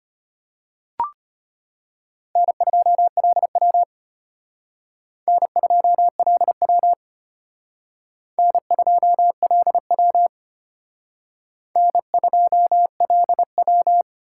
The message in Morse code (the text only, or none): N2LW N2LW N2LW N2LW